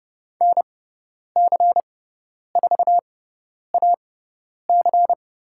Morse code, heard as a keyed tone, about 30 wpm, spaced as for 10 wpm, 700 Hz. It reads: NC4AC
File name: Lesson 18g - Letter m - Call Signs 30wpm